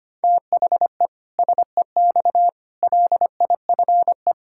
Morse code, heard as a keyed tone, about 25 wpm, 700 Hz. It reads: THE SEX LIFE
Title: Pangram Challenge v2 - 25wpm